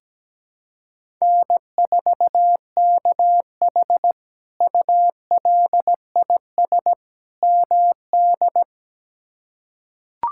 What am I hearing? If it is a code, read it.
Morse code, heard as a keyed tone, about 17 wpm, 700 Hz. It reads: N4KH ULIS MD
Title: K1USN SST QSO Practice - 17wpm